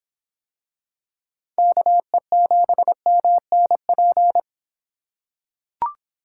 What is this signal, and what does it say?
Morse code, heard as a keyed tone, about 25 wpm, 700 Hz. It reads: KE7MNP